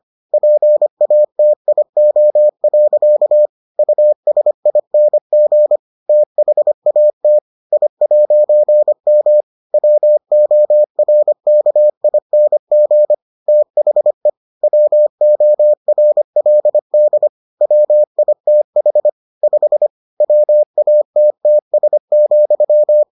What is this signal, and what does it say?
Morse code, heard as a keyed tone, about 25 wpm, 600 Hz. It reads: PATIO. USING THAT I'M WORKING THE WORLD WITH 5 WATTS,